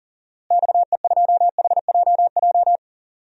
Morse code, heard as a keyed tone, about 40 wpm, 700 Hz. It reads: XE2HJJ
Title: DXCC-100 Call Signs - 6 Characters - Repeated 3x - 40wpm